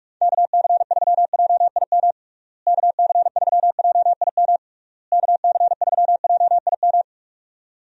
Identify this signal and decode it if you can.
Morse code, heard as a keyed tone, about 42 wpm, 700 Hz. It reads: KC3JIM KC3JIM KC3JIM
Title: USA Call Signs - 6 Characters - Repeated 3x - 45wpm